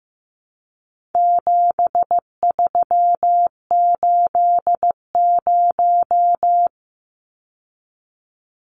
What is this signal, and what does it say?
Morse code, heard as a keyed tone, about 15 wpm, 700 Hz. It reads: 7380
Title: Numbers - 4-Digit 15wpm